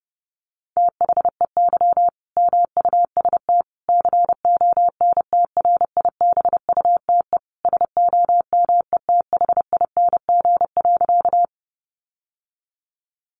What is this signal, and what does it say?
Morse code, heard as a keyed tone, about 30 wpm, 700 Hz. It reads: THEY MUST CONTRIBUTE SOMETHING.
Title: Sentences from Top 1400 Words 30wpm